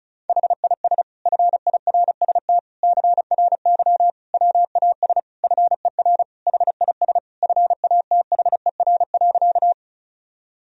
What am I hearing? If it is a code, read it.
Morse code, heard as a keyed tone, about 36 wpm, 700 Hz. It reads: HIS FIRST CRY WAS FER HIS FATHER.
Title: Sentences from Top 500 Words and Abbreviations 35wpm